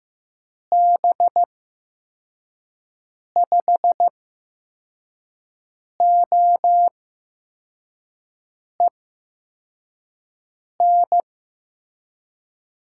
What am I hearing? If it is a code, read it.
Morse code, heard as a keyed tone, about 15 wpm, spaced as for 4 wpm, 700 Hz. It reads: B5OEN
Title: USA Call Signs - 6 Characters - 8x Character Spacing - 15wpm